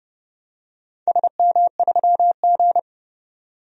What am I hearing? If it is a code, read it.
Morse code, heard as a keyed tone, about 29 wpm, 700 Hz. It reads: SM3G
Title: DXCC-100 Call Signs - 4 Characters - 30wpm